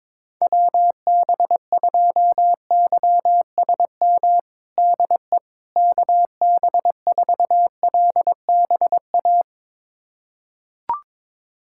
Morse code, heard as a keyed tone, about 22 wpm, 700 Hz. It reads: WB2YSM DE KB4LBA
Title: USA Callsign DE Callsign 22wpm